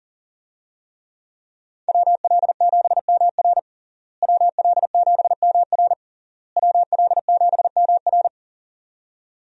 Morse code, heard as a keyed tone, about 40 wpm, 700 Hz. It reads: WL7MR WL7MR WL7MR
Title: USA Call Signs - 5 Characters - Repeated 3x - 40wpm